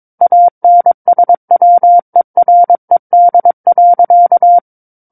Morse code, heard as a keyed tone, about 22 wpm, 700 Hz. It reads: ANSWERED.
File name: Book One / 13-Chapter 13 - 22 WPM